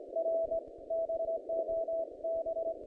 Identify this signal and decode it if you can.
Morse code, heard as a keyed tone, about 21 wpm, 660 Hz, with noise. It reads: R BOB
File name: contests / naqp-cw-k5tr-op-k5pi- -0100